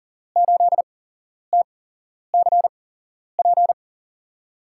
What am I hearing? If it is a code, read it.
Morse code, heard as a keyed tone, about 38 wpm, spaced as for 11 wpm, 700 Hz. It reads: 8TCP
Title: USA Call Signs - 5 Characters - 8x Character Spacing - 40wpm